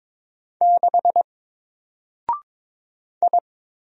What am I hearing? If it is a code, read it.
Morse code, heard as a keyed tone, about 22 wpm, 700 Hz. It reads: 6 I
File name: Single-Letter-Number-22wpm